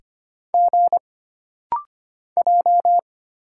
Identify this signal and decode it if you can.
Morse code, heard as a keyed tone, about 25 wpm, 700 Hz. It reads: G J
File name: Single Character (Letter:Number:Punctuation:Prosign) - Rapid-Fire - 25wpm